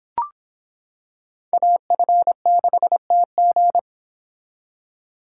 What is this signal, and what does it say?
Morse code, heard as a keyed tone, about 25 wpm, 700 Hz. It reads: AF6TG